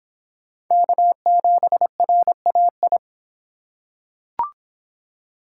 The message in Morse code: K7RAI

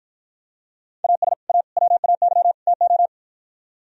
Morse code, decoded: AS A WAY TO

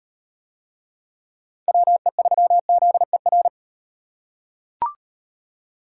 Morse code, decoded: WE3ZER

